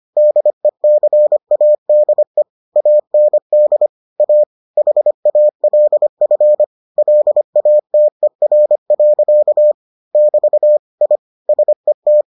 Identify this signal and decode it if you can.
Morse code, heard as a keyed tone, about 25 wpm, 600 Hz. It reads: DECADE AND A HALF LATER. <BT> I SET